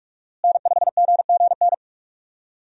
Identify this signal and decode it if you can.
Morse code, heard as a keyed tone, about 46 wpm, 700 Hz. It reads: N5GGN
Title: USA Call Signs - 5 Characters - Repeated 3x - 45wpm